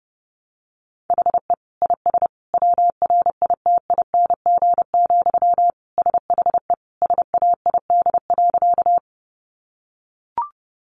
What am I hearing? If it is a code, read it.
Morse code, heard as a keyed tone, about 30 wpm, 700 Hz. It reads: HE IS WRITING, SHE SAID.
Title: Sentences from Top 800 Words 30wpm